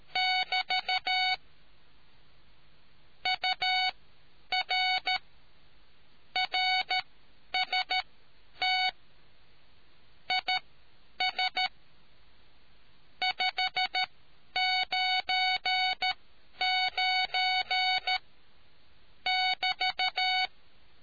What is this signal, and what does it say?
Morse code, heard as a keyed tone, about 13 wpm, spaced as for 9 wpm, 2250 Hz. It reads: <BT> UR RST IS 599 <BT>